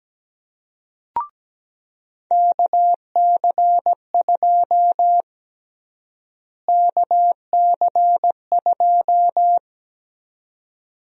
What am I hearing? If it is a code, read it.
Morse code, heard as a keyed tone, about 17 wpm, 700 Hz. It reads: KC2 KC2
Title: USA Call Sign Prefixes - Repeated 2x - 17wpm